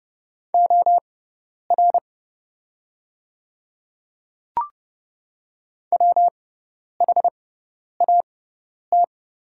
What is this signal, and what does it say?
Morse code, heard as a keyed tone, about 30 wpm, spaced as for 10 wpm, 700 Hz. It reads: OR WHAT
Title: 6x Character Spacing - Top 300 Words - 30wpm